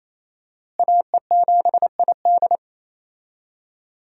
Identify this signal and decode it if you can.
Morse code, heard as a keyed tone, about 28 wpm, 700 Hz. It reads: AE7ID
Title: USA Call Signs - 5 Characters - Repeated 3x - 28wpm